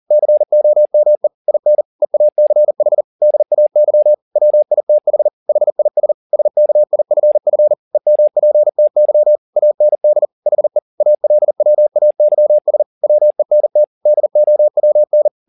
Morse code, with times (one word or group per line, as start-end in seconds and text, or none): 0.1-1.3 COME
1.5-1.8 IN
2.0-3.0 EACH
3.2-4.1 DAY
4.4-5.3 WITH
5.5-6.1 HIS
6.3-7.7 SKIFF
7.9-9.4 EMPTY
9.6-10.3 AND
10.5-10.8 HE
11.0-12.8 ALWAYS
13.0-13.8 WENT
14.1-15.3 DOWN